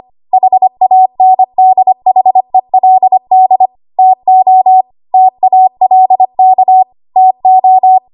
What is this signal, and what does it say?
Morse code, heard as a keyed tone, about 25 wpm, 750 Hz. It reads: HANDHELD TO TALK TO
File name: Archive / 25 WPM / 25WPM